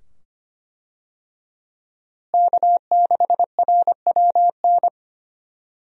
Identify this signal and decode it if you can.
Morse code, heard as a keyed tone, about 25 wpm, 700 Hz. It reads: K6RWN
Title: USA Call Signs - 5 Characters - Repeated 2x - 25wpm